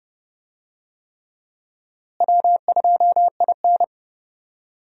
Morse code, heard as a keyed tone, about 30 wpm, 700 Hz. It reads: W2IN
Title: USA Call Signs - 4 Characters - Repeated 3x - 30wpm